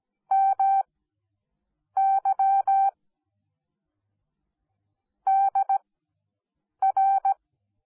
Morse code, heard as a keyed tone, about 17 wpm, spaced as for 7 wpm, 780 Hz. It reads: MY DR
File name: Onlinelehrgang / morsen / DJ4UF Morsen10